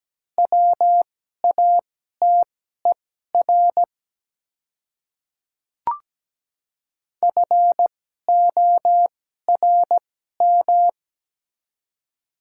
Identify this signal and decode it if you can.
Morse code, heard as a keyed tone, about 17 wpm, spaced as for 12 wpm, 700 Hz. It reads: WATER FORM
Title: 2x Character Spacing - Top 300 Words - 17wpm